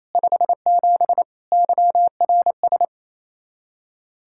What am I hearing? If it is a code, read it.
Morse code, heard as a keyed tone, about 37 wpm, 700 Hz. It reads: 57 YRS